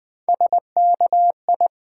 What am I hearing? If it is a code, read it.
Morse code, heard as a keyed tone, about 20 wpm, 700 Hz. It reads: SKI
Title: Build A Word - 1 Letter At A Time - 4-Letter Words - 20wpm